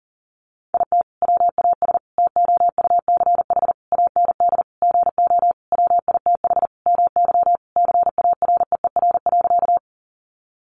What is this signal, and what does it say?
Morse code, heard as a keyed tone, about 40 wpm, 700 Hz. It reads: IT WAS TOUCH AND GO WITH MY CAREER.
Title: Sentences from Top 1200 Words 40wpm